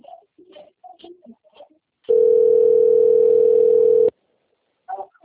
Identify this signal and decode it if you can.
Ringback tone (North America), 440 + 480 Hz.
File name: rg-1600-unknown- - - .7676